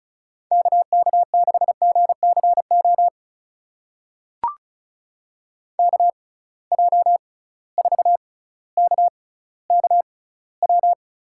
Morse code, heard as a keyed tone, about 36 wpm, spaced as for 18 wpm, 700 Hz. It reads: KK6GCO KJ4KKW
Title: USA Call Signs - 6 Characters - 6x Character Spacing - 35wpm